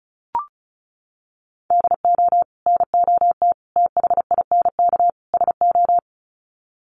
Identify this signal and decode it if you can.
Morse code, heard as a keyed tone, about 34 wpm, 700 Hz. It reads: DO NOT THINK SO
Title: Sets of 4 Words 35wpm